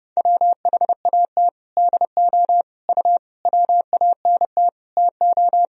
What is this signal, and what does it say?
Morse code, heard as a keyed tone, about 30 wpm, 700 Hz. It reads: WHAT DO U WANT TO
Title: Sentences from Top 200 Words and Abbreviations 30wpm